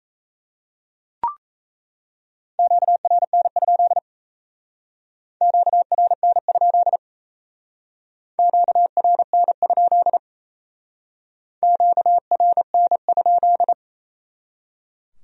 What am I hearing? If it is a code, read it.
Morse code, heard as a keyed tone, about 36 wpm, 700 Hz. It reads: QRN? QRN? QRN? QRN?